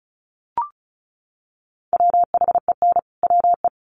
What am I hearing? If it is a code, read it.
Morse code, heard as a keyed tone, about 34 wpm, 700 Hz. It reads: WHEN WE
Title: Sets of 2 Words 35wpm